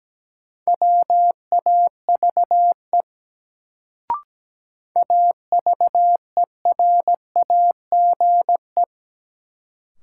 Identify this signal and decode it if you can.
Morse code, heard as a keyed tone, about 17 wpm, 700 Hz. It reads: WAVE AVERAGE
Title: Sentences from Top 1400 Words - New Word Drill Alt - 17wpm